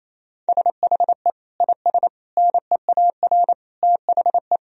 Morse code, heard as a keyed tone, about 27 wpm, 700 Hz. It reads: SHE IS NEAR THE
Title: Sentences from Top 1000 Words - Review Part 1 - 28wpm